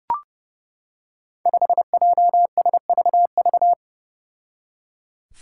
Morse code, heard as a keyed tone, about 30 wpm, 700 Hz. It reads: 5JSVV
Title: Five Challenging Letter-Number Combinations 30wpm